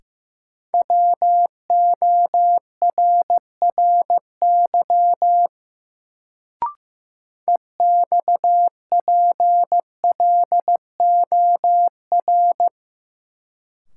Morse code, heard as a keyed tone, about 15 wpm, 700 Hz. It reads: WORRY EXPLOR
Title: Sentences from Top 1100 Words - New Word Drill Alt - 15wpm